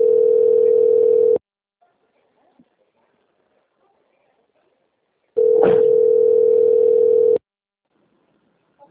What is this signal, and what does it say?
Ringback tone (North America), 440 + 480 Hz.
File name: out- - - - - .2309